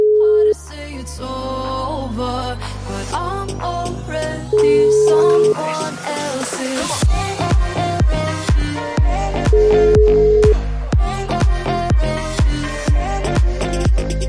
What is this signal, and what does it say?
Ringback tone (Europe, 425 Hz), with noise.